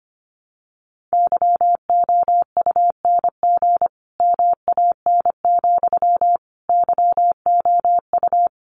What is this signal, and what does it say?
Morse code, heard as a keyed tone, about 25 wpm, 700 Hz. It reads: YOUNG MAN, YOU
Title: Sentences from Top 400 Words 25wpm